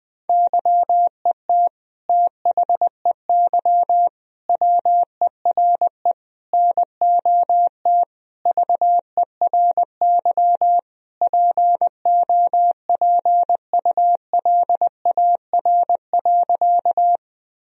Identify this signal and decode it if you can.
Morse code, heard as a keyed tone, about 21 wpm, 700 Hz. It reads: YET THEY WERE NOT VERY POPULAR.
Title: Sentences from Top 1000 Words - Review Part 3 - 20wpm